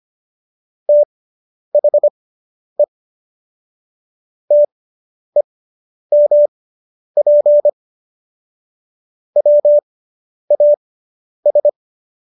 Morse code, heard as a keyed tone, about 25 wpm, spaced as for 10 wpm, 590 Hz. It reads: THE TEMP WAS